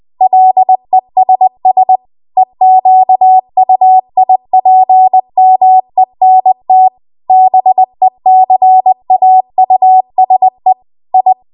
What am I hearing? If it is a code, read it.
Morse code, heard as a keyed tone, about 20 wpm, 750 Hz. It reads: LESS EQUIPMENT BECAUSE I
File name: Archive / 20 WPM / 20WPM